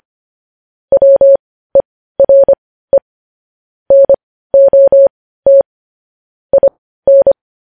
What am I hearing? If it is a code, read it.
Morse code, heard as a keyed tone, about 25 wpm, spaced as for 15 wpm, 550 Hz. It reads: WERE NOT IN
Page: news-headlines-in-morse-code-at-15-wpm